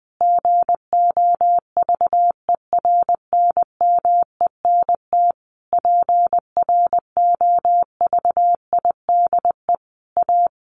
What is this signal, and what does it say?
Morse code, heard as a keyed tone, about 20 wpm, 700 Hz. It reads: GOVERNMENT PROVIDE A
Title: Sentences from Top 1500 Words 20wpm